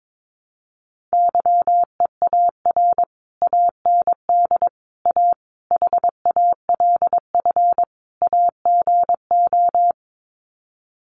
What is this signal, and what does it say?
Morse code, heard as a keyed tone, about 22 wpm, 700 Hz. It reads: YEAR AND A HALF AGO
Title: Sets of 5 Words Encore 22wpm